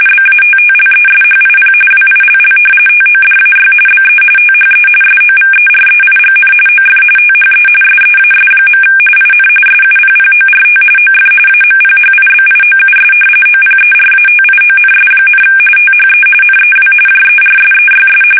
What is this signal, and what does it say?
RTTY, about 45.45 baud. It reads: BATTLESHIP WILL TODAY BE THE SOVEREIGN OF THE SEAS. IN SUPPORT OF AMPHIBIOUS OPERATIONS, THE NEW JERSEYS
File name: fsk850 / battleshipnj